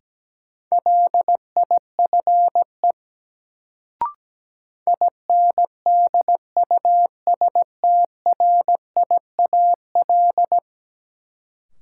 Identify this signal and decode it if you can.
Morse code, heard as a keyed tone, about 17 wpm, 700 Hz. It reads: LIFE INDUSTRIAL
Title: Sets of 5 Words - New Word Drill - 17wpm